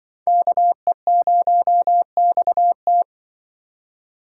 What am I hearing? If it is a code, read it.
Morse code, heard as a keyed tone, about 24 wpm, 700 Hz. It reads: KE0XT
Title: Speed Racing - USA 2x2 Call Signs 20wpm